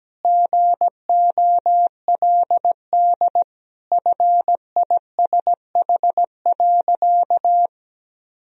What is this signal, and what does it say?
Morse code, heard as a keyed tone, about 17 wpm, 700 Hz. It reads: GOLD FISH.
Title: Sentences from Top 1000 Words - Review Part 3 - 17wpm